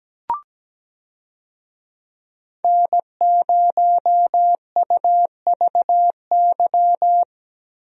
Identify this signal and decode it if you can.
Morse code, heard as a keyed tone, about 17 wpm, 700 Hz. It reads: N0UVY
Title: USA Call Signs - 5 Characters - Repeated 2x - 17wpm